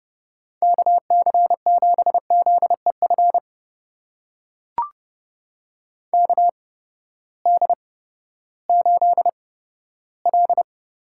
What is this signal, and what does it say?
Morse code, heard as a keyed tone, about 30 wpm, spaced as for 15 wpm, 700 Hz. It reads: KC7ZEF KD8L
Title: USA Call Signs - 6 Characters - 8x Character Spacing - 30wpm